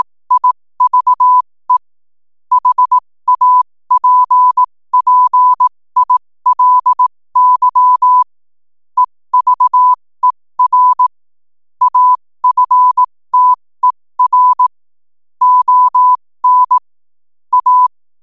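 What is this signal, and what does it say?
Morse code, heard as a keyed tone, about 18 wpm, spaced as for 15 wpm, 1000 Hz. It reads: IVE HAPPILY EVER AFTER ON A
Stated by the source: 15 wpm Morse code from RNZN Communicators Association.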